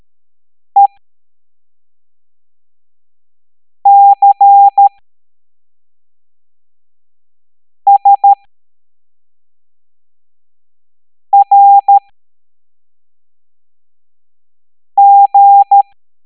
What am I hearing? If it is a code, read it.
Morse code, heard as a keyed tone, about 13 wpm, spaced as for 3 wpm, 800 Hz. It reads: ECSRG